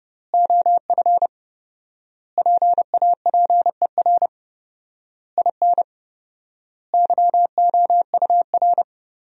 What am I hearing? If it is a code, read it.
Morse code, heard as a keyed tone, about 30 wpm, 700 Hz. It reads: OF PAPER IN YOUR
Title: 4x Word-Spacing - Sentences from Top 400 Words - 30wpm